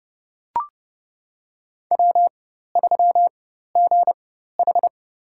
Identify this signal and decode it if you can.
Morse code, heard as a keyed tone, about 29 wpm, spaced as for 14 wpm, 700 Hz. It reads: W3GH
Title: USA Call Signs - 4 Characters - 4x Character Spacing - 30wpm